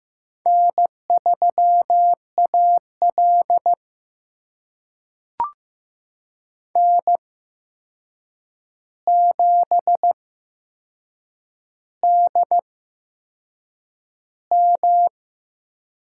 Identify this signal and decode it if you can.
Morse code, heard as a keyed tone, about 15 wpm, spaced as for 6 wpm, 700 Hz. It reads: N3AL N7DM
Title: USA Call Signs - 4 Characters - 8x Character Spacing - 15wpm